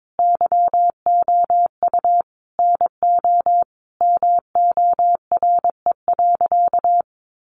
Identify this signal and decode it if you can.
Morse code, heard as a keyed tone, about 22 wpm, 700 Hz. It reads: YOU NO MORE.
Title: Sentences from Top 1200 Words 22wpm